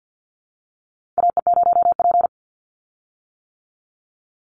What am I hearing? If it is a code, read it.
Morse code, heard as a keyed tone, about 50 wpm, 700 Hz. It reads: AE0P
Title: Call Signs - 50wpm